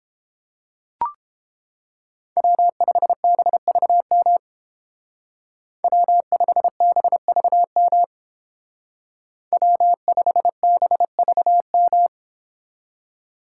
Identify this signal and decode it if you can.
Morse code, heard as a keyed tone, about 29 wpm, 700 Hz. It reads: W5BVM W5BVM W5BVM